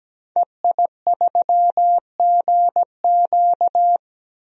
Morse code, heard as a keyed tone, about 17 wpm, 700 Hz. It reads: EI3GQ
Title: DXCC-100 Call Signs - 5 Characters - 17wpm